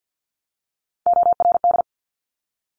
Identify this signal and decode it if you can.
Morse code, heard as a keyed tone, about 50 wpm, 700 Hz. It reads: ORD
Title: Top Three Letter Combinations in Words 50wpm